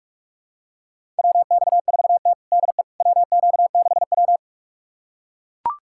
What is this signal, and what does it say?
Morse code, heard as a keyed tone, about 42 wpm, 700 Hz. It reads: WX4T DE WQ6W
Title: USA Callsign DE Callsign - 4 Characters - 45wpm